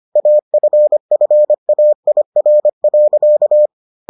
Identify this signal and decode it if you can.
Morse code, heard as a keyed tone, about 25 wpm, 600 Hz. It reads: AFFAIR.